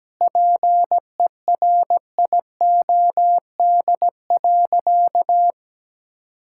Morse code, heard as a keyed tone, about 17 wpm, 700 Hz. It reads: PERIOD.
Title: Sentences from Top 1000 Words - Review Part 5 - 17wpm